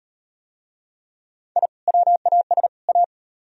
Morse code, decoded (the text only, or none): I WAS A